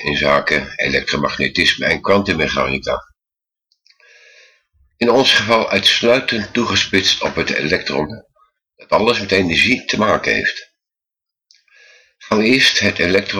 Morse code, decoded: none